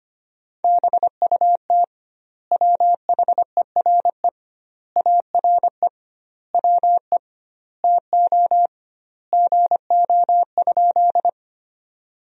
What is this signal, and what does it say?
Morse code, heard as a keyed tone, about 25 wpm, 700 Hz. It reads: BUT WHERE ARE WE TO GO?
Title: 2x Word-Spacing - Sentences from Top 200 Words - 25wpm